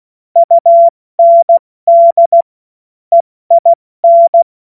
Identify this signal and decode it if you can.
Morse code, heard as a keyed tone, about 16 wpm, spaced as for 14 wpm, 680 Hz. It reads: UND EIN
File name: Morsekurs / Witze / Witz 12. April 2022 70 F80